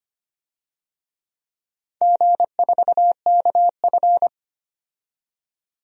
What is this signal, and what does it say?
Morse code, heard as a keyed tone, about 25 wpm, 700 Hz. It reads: G4KF
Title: DXCC-100 Call Signs - 4 Characters - 25wpm